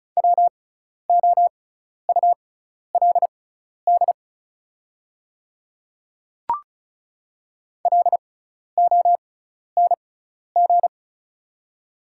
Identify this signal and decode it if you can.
Morse code, heard as a keyed tone, about 36 wpm, spaced as for 12 wpm, 700 Hz. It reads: WOULD LONG
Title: 6x Character Spacing - Top 200 Words - 35wpm